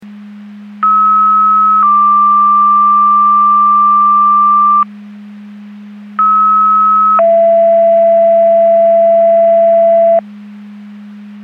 Two-tone page, heard as a one-second tone, then a three-second tone, about 1290 then 1180 Hz, about 1290 then 670 Hz. It is heard from 0.8 to 10.2 s.
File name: Portage County 154.3850 Fire EMS Paging